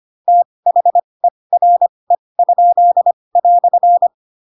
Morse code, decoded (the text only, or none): THERE?"